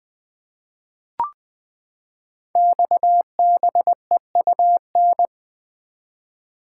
Morse code, heard as a keyed tone, about 20 wpm, 700 Hz. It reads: XBEUN
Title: Code Groups - 5 Letters - 20wpm